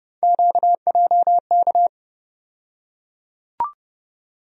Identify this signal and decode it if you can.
Morse code, heard as a keyed tone, about 30 wpm, 700 Hz. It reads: QJK
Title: USA Call Sign Postfixes - 3 Letters 30wpm